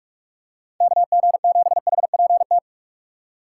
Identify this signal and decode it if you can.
Morse code, heard as a keyed tone, about 45 wpm, 700 Hz. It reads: KG7HPT
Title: USA Call Signs - 4x Character Spacing - 45wpm